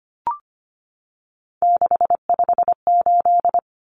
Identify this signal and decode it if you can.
Morse code, heard as a keyed tone, about 25 wpm, 700 Hz. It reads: 658
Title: Numbers - 3-Digit 25wpm